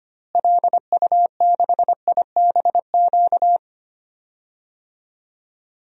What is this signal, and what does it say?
Morse code, heard as a keyed tone, about 25 wpm, 700 Hz. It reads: LU6IBQ
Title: DXCC-100 Call Signs - 6 Characters - Repeated 2x - 25wpm